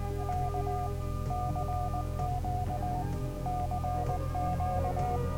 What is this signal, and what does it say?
Morse code, heard as a keyed tone, about 19 wpm, 730 Hz. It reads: K CQ CQ